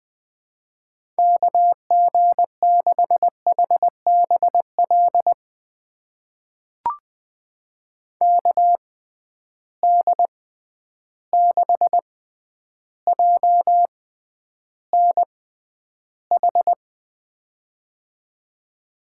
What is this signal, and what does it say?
Morse code, heard as a keyed tone, about 21 wpm, spaced as for 10 wpm, 700 Hz. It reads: KG6HBL KD6JNH
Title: USA Call Signs - 6x Character Spacing - 20wpm